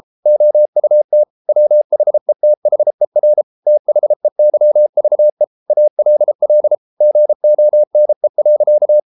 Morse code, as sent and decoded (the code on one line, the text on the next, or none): --- ..- - / .-- .... - .... .-. / - .... -.-- ...- . / .- .-.. .-.. / --. --- -. .-.-.-
OUT WHETHER THEYVE ALL GONE.